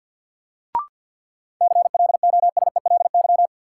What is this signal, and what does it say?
Morse code, heard as a keyed tone, about 50 wpm, 700 Hz. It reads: CLOSELY
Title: Sentences from Top 1400 Words - New Word Drill - 50wpm